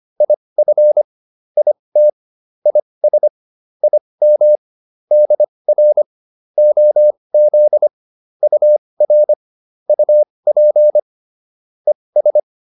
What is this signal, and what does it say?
Morse code, heard as a keyed tone, about 25 wpm, spaced as for 20 wpm, 600 Hz. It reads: IF IT IS IM DR OZ UR UP ES